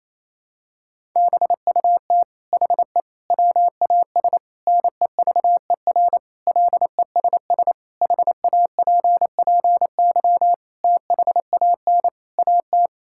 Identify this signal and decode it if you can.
Morse code, heard as a keyed tone, about 28 wpm, 700 Hz. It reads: BUT HE WAS NEVER LESS HAPPY THAN AT
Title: Sentences from Top 1000 Words - Review Part 5 - 28wpm